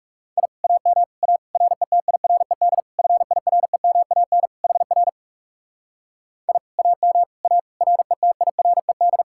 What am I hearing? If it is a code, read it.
Morse code, heard as a keyed tone, about 37 wpm, 700 Hz. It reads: I AM A RETIRED FIREMAN HR I AM A RETIRED
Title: Speed-Racing Common QSO Phrases - 30wpm